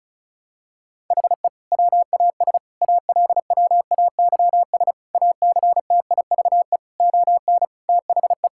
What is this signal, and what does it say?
Morse code, heard as a keyed tone, about 36 wpm, 700 Hz. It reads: HE WAS ALWAYS ACTIVE ON THE
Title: Sentences from Top 1000 Words - Review Part 6 - 35wpm